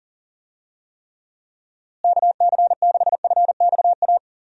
Morse code, heard as a keyed tone, about 40 wpm, 700 Hz. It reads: KC6FXA
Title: USA Call Signs - 6 Characters - Repeated 2x - 40wpm